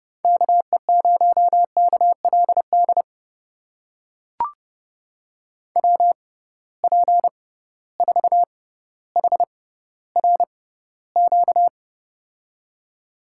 Morse code, heard as a keyed tone, about 30 wpm, spaced as for 15 wpm, 700 Hz. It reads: KE0KLD WP4HRQ
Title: USA Call Signs - 6 Characters - 6x Character Spacing - 30wpm